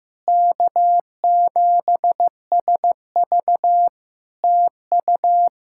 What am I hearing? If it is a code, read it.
Morse code, heard as a keyed tone, about 15 wpm, 700 Hz. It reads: K7SV TU
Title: Practice QSO - Exchange 4 - Ending - 15wpm